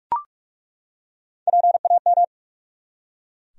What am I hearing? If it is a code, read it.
Morse code, heard as a keyed tone, about 41 wpm, 700 Hz. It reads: PAM
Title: USA Common Names - Max 3 Letters - 45wpm